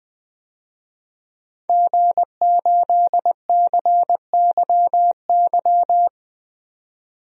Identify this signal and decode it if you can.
Morse code, heard as a keyed tone, about 20 wpm, 700 Hz. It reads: G8CYY
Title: DXCC-100 Call Signs - 5 Characters - 20wpm